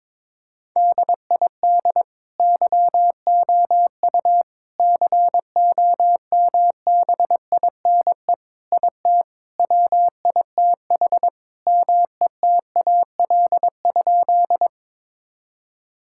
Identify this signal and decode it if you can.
Morse code, heard as a keyed tone, about 22 wpm, 700 Hz. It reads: DID YOU COMBINE IT WITH METAL?
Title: Sentences from Top 1600 Words 22wpm